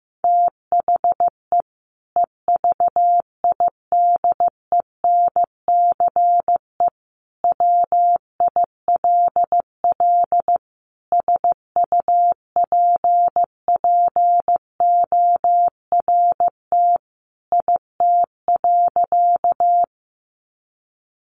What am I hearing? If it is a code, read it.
Morse code, heard as a keyed tone, about 15 wpm, 700 Hz. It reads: THE EVIDENCE WILL SUPPORT IT.
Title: Sentences from Top 1300 Words 15wpm